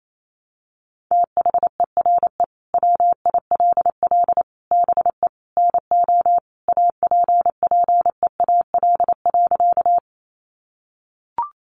Morse code, heard as a keyed tone, about 28 wpm, 700 Hz. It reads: THERE WILL BE NO APPEAL.
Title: Sentences from Top 1300 Words 28wpm